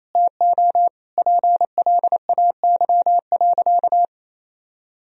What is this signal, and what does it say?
Morse code, heard as a keyed tone, about 28 wpm, 700 Hz. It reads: TO PLAY.
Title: Sentences from Top 1000 Words - Review Part 3 - 28wpm